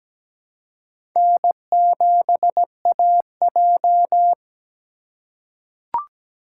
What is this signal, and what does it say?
Morse code, heard as a keyed tone, about 17 wpm, 700 Hz. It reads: N7AJ
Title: USA Call Signs - 4 Characters - 4x Character Spacing - 17wpm